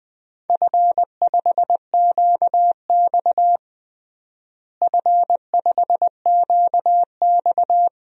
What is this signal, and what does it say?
Morse code, heard as a keyed tone, about 20 wpm, 700 Hz. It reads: F5QX F5QX
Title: DXCC-100 Call Signs - 4 Characters - Repeated 2x - 20wpm